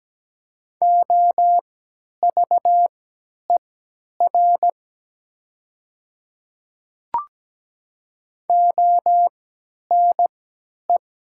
Morse code, heard as a keyed tone, about 17 wpm, spaced as for 10 wpm, 700 Hz. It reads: OVER ONE
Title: 3x Character Spacing - Top 300 Words - 17wpm